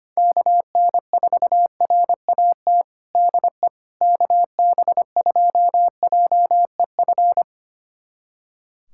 Morse code, heard as a keyed tone, about 25 wpm, 700 Hz. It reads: KN4RAT DE KB2JEF
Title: USA Callsign DE Callsign - 6 Characters - 25wpm